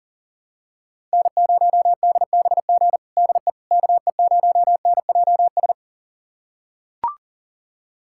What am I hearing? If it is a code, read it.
Morse code, heard as a keyed tone, about 40 wpm, 700 Hz. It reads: N0DBG DE KE0NJS
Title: USA Callsign DE Callsign 40wpm